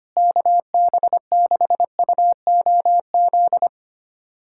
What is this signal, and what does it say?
Morse code, heard as a keyed tone, about 25 wpm, 700 Hz. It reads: KB6UOZ
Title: USA Callsign DE Callsign 25wpm